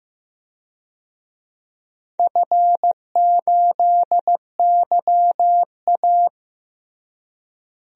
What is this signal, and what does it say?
Morse code, heard as a keyed tone, about 16 wpm, 700 Hz. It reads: F8YA
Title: DXCC-100 Call Signs - 4 Characters - Repeated 2x - 15wpm